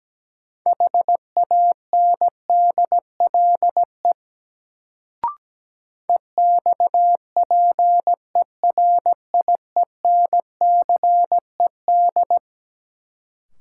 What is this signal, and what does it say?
Morse code, heard as a keyed tone, about 17 wpm, 700 Hz. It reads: HANDLE EXPERIENCED
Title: Sentences from Top 1100 Words - New Word Drill - 17wpm